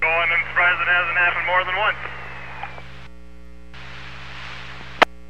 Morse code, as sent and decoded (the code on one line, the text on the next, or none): none